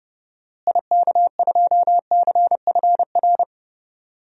Morse code, heard as a keyed tone, about 30 wpm, 700 Hz. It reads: IK2CFR
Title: DXCC-100 Call Signs - 6 Characters - Repeated 2x - 30wpm